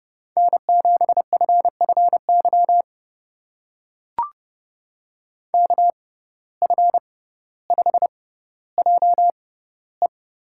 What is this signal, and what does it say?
Morse code, heard as a keyed tone, about 30 wpm, spaced as for 15 wpm, 700 Hz. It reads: N7FFY KF5JE
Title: USA Call Signs - 5 Characters - 6x Character Spacing - 30wpm